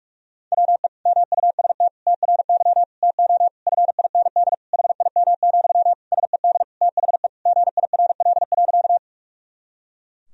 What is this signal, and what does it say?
Morse code, heard as a keyed tone, about 46 wpm, 700 Hz. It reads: WE MUST TRY TO FIND HIM, SED THE GIRL.
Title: Sentences from Top 400 Words and Abbreviations 45wpm